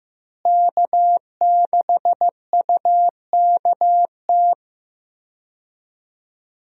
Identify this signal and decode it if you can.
Morse code, heard as a keyed tone, about 15 wpm, 700 Hz. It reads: K6UKT
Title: USA Call Signs - 5 Characters - Repeated 3x - 15wpm